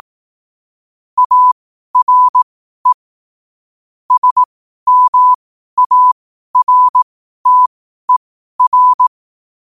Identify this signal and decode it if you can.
Morse code, heard as a keyed tone, about 18 wpm, spaced as for 12 wpm, 1000 Hz. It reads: ARE SMARTER
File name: qotdfiles / 12wpm